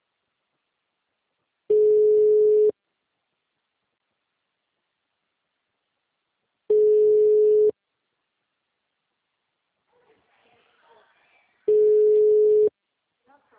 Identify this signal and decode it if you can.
Ringback tone (Europe, 425 Hz).
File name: out- -2024- - - .6529